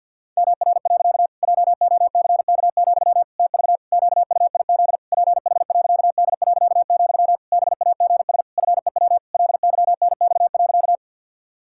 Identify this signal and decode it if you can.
Morse code, heard as a keyed tone, about 47 wpm, 700 Hz. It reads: MR. JOCK, TV QUIZ PH.D., BAGS FEW LYNX.
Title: Pangram Challenge v2 - 50wpm